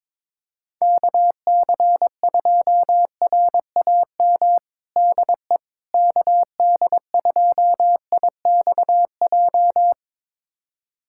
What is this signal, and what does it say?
Morse code, heard as a keyed tone, about 22 wpm, 700 Hz. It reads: KC2RAM DE KD2IXJ
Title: USA Callsign DE Callsign - 6 Characters - 22wpm